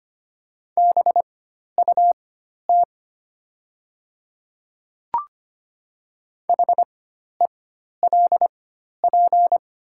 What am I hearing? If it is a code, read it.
Morse code, heard as a keyed tone, about 25 wpm, spaced as for 12 wpm, 700 Hz. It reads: BUT HELP
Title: 4x Character Spacing - Top 300 Words - 25wpm